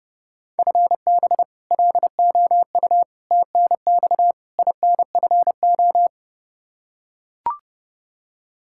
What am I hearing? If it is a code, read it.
Morse code, heard as a keyed tone, about 30 wpm, 700 Hz. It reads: FB LOU TNX INFO
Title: K1USN SST QSO Practice 30wpm